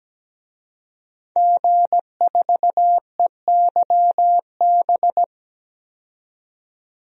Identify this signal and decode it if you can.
Morse code, heard as a keyed tone, about 17 wpm, 700 Hz. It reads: G4EYB
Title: DXCC-100 Call Signs - 5 Characters - 17wpm